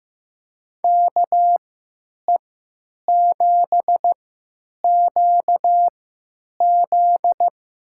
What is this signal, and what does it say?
Morse code, heard as a keyed tone, about 15 wpm, spaced as for 8 wpm, 700 Hz. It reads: KE7QZ
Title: USA Call Signs - 3x Character Spacing - 15wpm